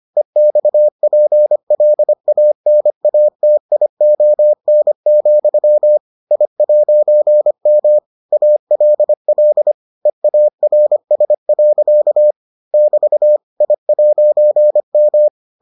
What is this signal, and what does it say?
Morse code, heard as a keyed tone, about 25 wpm, 600 Hz. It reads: EXPLANATION, I'M ALL EARS. <BT> I'M